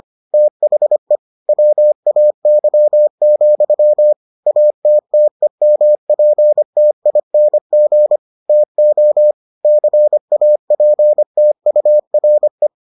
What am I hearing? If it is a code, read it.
Morse code, heard as a keyed tone, about 25 wpm, 600 Hz. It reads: THE WAY, ATTEMPTING TO CAPTURE